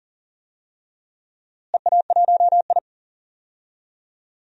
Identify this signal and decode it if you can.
Morse code, heard as a keyed tone, about 40 wpm, 700 Hz. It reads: EA1I
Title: DXCC-100 Call Signs - 4 Characters - Repeated 3x - 40wpm